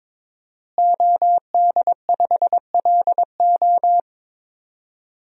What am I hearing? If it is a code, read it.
Morse code, heard as a keyed tone, about 22 wpm, 700 Hz. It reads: OD5LO
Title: DXCC-100 Call Signs - 5 Characters - Repeated 3x - 22wpm